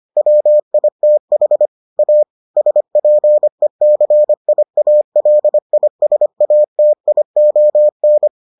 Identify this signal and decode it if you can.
Morse code, heard as a keyed tone, about 25 wpm, 600 Hz. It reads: WITH A SPECIALISATION